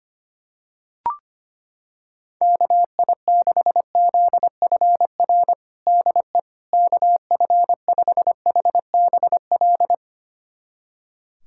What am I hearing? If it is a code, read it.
Morse code, heard as a keyed tone, about 25 wpm, 700 Hz. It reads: KI6ZFR DE KF5HBL
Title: USA Callsign DE Callsign - 6 Characters - 25wpm